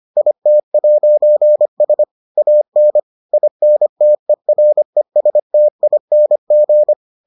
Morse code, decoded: IT'S AN INTERESTING

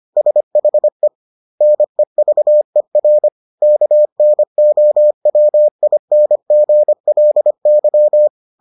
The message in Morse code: SHE NEVER KNOWINGLY

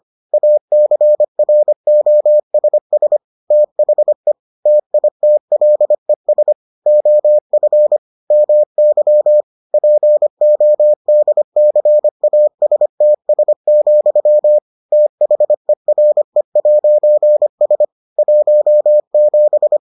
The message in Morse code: ACROSS THE TITLES OF MY PODCASTS, THERE'S 17

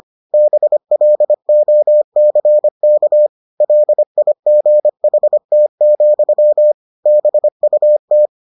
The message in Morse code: BLOCK LIGHT, BUT

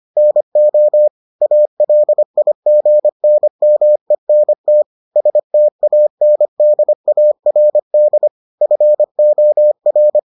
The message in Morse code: NO ALIGNMENT STANDARD FOR

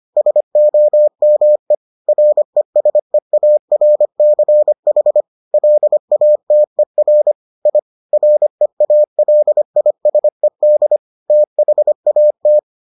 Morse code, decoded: SOME RESEARCH LATER I REALISED THAT

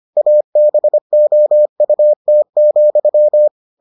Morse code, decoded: ABOUT,

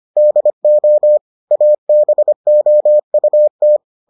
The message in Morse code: DO ABOUT

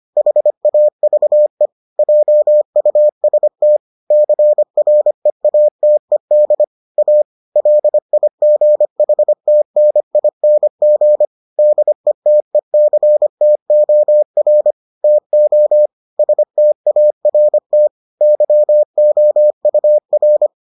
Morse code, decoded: HAVE JUST CREATED A LIGHTNING DETECTOR TO START YOUR

